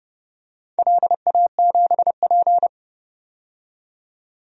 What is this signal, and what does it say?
Morse code, heard as a keyed tone, about 30 wpm, 700 Hz. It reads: LA7P
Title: DXCC-100 Call Signs - 4 Characters - 30wpm